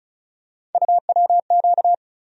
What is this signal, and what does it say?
Morse code, heard as a keyed tone, about 36 wpm, 700 Hz. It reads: UWQ
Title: USA Call Sign Postfixes - 3 Letters 35wpm